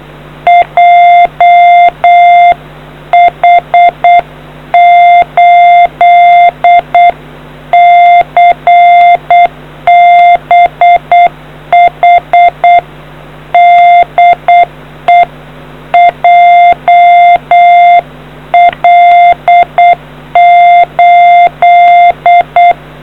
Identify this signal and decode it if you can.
Morse code, heard as a keyed tone, about 8 wpm, 700 Hz. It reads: JH8CBH DE JL8